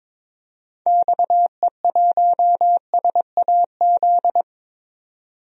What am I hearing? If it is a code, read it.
Morse code, heard as a keyed tone, about 22 wpm, 700 Hz. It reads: XE1SAZ
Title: DXCC-100 Call Signs - 6 Characters - 22wpm